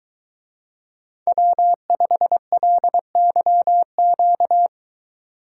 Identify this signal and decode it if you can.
Morse code, heard as a keyed tone, about 23 wpm, 700 Hz. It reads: W5LYQ